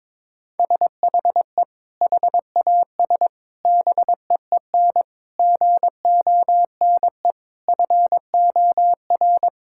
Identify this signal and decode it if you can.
Morse code, heard as a keyed tone, about 22 wpm, 700 Hz. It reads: SHE HAS BEEN GONE FOR
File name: practice / 22wpm / Sentences with 3-Word Phrases - 22wpm